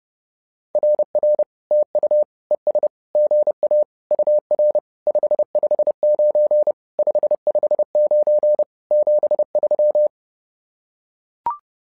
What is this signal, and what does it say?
Morse code, heard as a keyed tone, about 30 wpm, 600 Hz. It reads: RR TU ES GA UR 559 559 73
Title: SOTA QSO Practice 30wpm